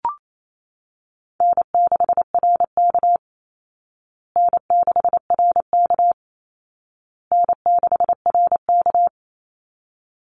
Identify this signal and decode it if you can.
Morse code, heard as a keyed tone, about 28 wpm, 700 Hz. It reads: N6RK N6RK N6RK
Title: Easy Call Signs Repeated 3x - 28wpm